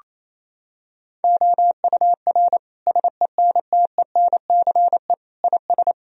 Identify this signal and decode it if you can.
Morse code, heard as a keyed tone, about 28 wpm, 700 Hz. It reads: OUR SENTENCE IS
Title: Sentences from Top 1000 Words - Review Part 1 - 28wpm